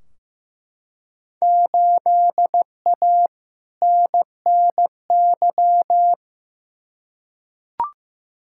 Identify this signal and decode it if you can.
Morse code, heard as a keyed tone, about 15 wpm, 700 Hz. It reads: 8A NNY
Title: Field Day Exchanges Encore - 15wpm